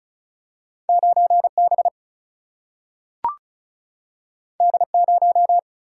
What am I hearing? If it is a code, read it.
Morse code, heard as a keyed tone, about 36 wpm, 700 Hz. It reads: 9 B D 0
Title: Single-Letter Single-Number 35wpm